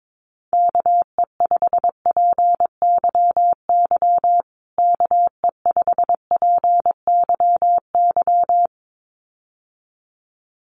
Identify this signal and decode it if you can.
Morse code, heard as a keyed tone, about 22 wpm, 700 Hz. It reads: KE5PYY KE5PYY
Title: USA Call Signs Repeated 2x 22wpm